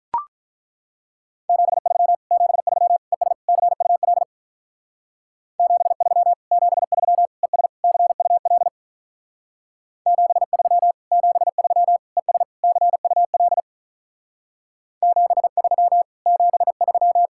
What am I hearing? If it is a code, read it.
Morse code, heard as a keyed tone, about 43 wpm, 700 Hz. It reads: 73 73 ES CUL 73 73 ES CUL 73 73 ES CUL 73 73
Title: Speed-Racing Common QSO Phrases - 35wpm